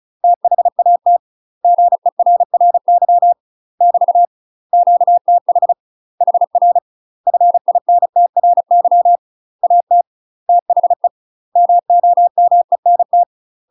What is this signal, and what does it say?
Morse code, heard as a keyed tone, about 35 wpm, 700 Hz. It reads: THAT GERRY <BT> QTH HR FINTRY AT THE MOMENT
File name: UK MASTER QSOs TRAINING / 35WPM / UK MASTER QSOs 0007